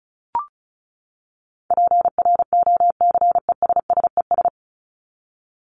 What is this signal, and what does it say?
Morse code, heard as a keyed tone, about 34 wpm, 700 Hz. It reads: PROCESSES
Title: ES:S Suffix - Top 100 Words with Base Words 35wpm